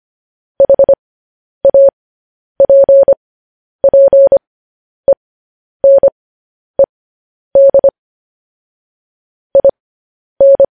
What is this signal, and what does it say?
Morse code, heard as a keyed tone, about 25 wpm, spaced as for 10 wpm, 550 Hz. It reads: HAPPENED IN
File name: podcast / News headlines-1x-25x10